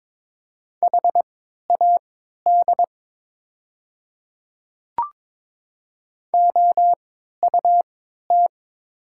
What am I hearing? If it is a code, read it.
Morse code, heard as a keyed tone, about 23 wpm, spaced as for 13 wpm, 700 Hz. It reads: HAD OUT
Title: 3x Character Spacing - Top 100 Words - 22wpm